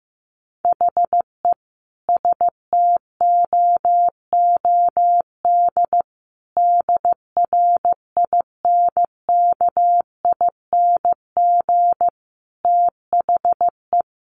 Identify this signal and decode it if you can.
Morse code, heard as a keyed tone, about 15 wpm, 700 Hz. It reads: HE STOOD DRINKING THE
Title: A Sound Of Thunder 15wpm